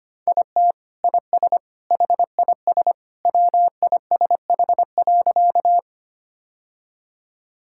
Morse code, heard as a keyed tone, about 25 wpm, 700 Hz. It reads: IT IS HIS WISH.
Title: Sentences from Top 1600 Words 25wpm